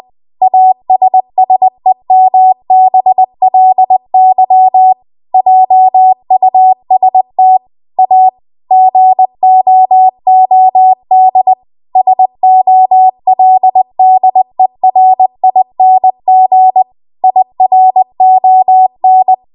Morse code, heard as a keyed tone, about 20 wpm, 750 Hz. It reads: ASSEMBLY JUST A GOOD SOLDERING IRON